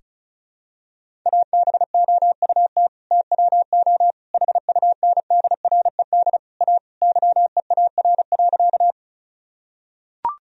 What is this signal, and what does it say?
Morse code, heard as a keyed tone, about 36 wpm, 700 Hz. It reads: ABOUT TWO HUNDRED A YEAR.
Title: Sentences from Top 1000 Words - Review Part 2 - 35wpm